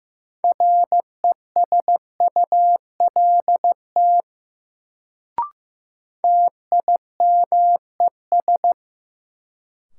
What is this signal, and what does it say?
Morse code, heard as a keyed tone, about 15 wpm, 700 Hz. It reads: RESULT TIMES
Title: Sets of 3 Words - New Word Drill - 15wpm